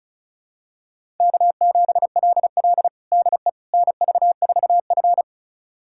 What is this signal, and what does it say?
Morse code, heard as a keyed tone, about 36 wpm, 700 Hz. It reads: K7LL DE NV4F
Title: USA Callsign DE Callsign - 4 Characters - 35wpm